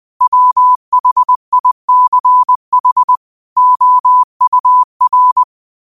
Morse code, heard as a keyed tone, about 20 wpm, 1000 Hz. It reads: WHICH OUR